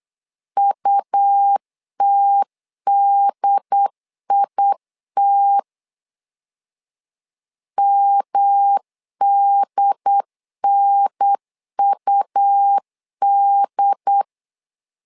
Morse code, heard as a keyed tone, about 8 wpm, 800 Hz. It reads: UTDIT MDNUD
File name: LEZIONI / 3 MWRO / 3 M 8wpm 2300